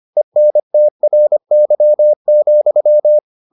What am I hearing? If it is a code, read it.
Morse code, heard as a keyed tone, about 25 wpm, 600 Hz. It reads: ENTRY,